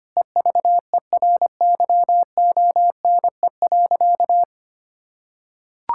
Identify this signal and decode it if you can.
Morse code, heard as a keyed tone, about 25 wpm, 700 Hz. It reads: EVERYONE.
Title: Sentences from Top 1000 Words - Review Part 5 - 25wpm